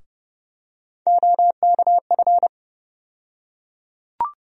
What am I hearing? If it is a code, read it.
Morse code, heard as a keyed tone, about 31 wpm, 700 Hz. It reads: OKF
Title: USA Call Sign Postfixes - 3 Letters 30wpm